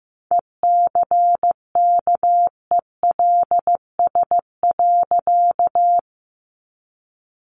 Morse code, heard as a keyed tone, about 15 wpm, 700 Hz. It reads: ECKELS.
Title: A Sound Of Thunder 15wpm